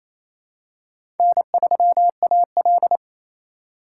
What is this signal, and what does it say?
Morse code, heard as a keyed tone, about 28 wpm, 700 Hz. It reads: N3AL
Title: USA Call Signs - 4 Characters - 4x Character Spacing - 28wpm